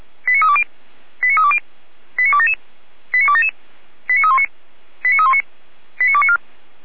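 Five-tone selective call (ZVEI), heard as quick runs of five tones, tones of about 70 ms. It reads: 89320 89320 89277 89277 89319 89319 89294
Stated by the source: ZVEI five-tone call 89276 received by SDR, from the Feuerwehr Orlamünde audio server.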